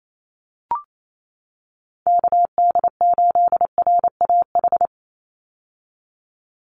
Morse code, heard as a keyed tone, about 28 wpm, 700 Hz. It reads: KD8RAH
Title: USA Call Signs 28wpm